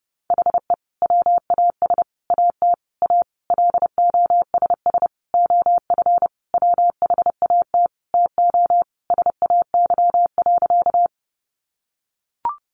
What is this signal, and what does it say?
Morse code, heard as a keyed tone, about 28 wpm, 700 Hz. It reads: HE WAS AT A LOSS OF WHAT TO SAY.
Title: Sentences from Top 1400 Words 30wpm